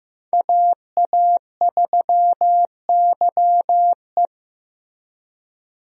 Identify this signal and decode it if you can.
Morse code, heard as a keyed tone, about 15 wpm, 700 Hz. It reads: AA3YE
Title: USA Call Signs - 5 Characters - Repeated 3x - 15wpm